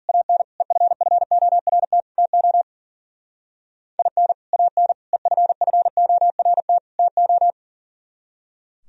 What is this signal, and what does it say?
Morse code, heard as a keyed tone, about 49 wpm, 700 Hz. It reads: AN EFFORT TO IN AN EFFORT TO
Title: Speed-Racing - Sets of 4 Words 40wpm